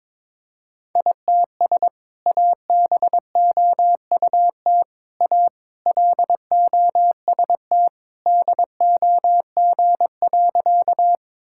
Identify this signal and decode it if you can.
Morse code, heard as a keyed tone, about 22 wpm, 700 Hz. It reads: ITS ABOUT A LOST DOG.
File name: practice / 22wpm / Sentences with 2-Word Phrases - 22wpm